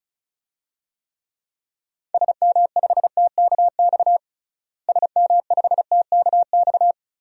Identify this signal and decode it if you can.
Morse code, heard as a keyed tone, about 36 wpm, 700 Hz. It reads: SM5TKX SM5TKX
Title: DXCC-100 Call Signs - 6 Characters - Repeated 2x - 35wpm